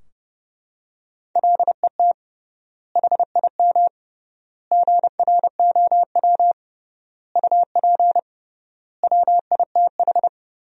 Morse code, heard as a keyed tone, about 32 wpm, 700 Hz. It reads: LET HIM GROW UP WITH
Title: 3x Word-Spacing - Sentences from Top 300 Words - 30wpm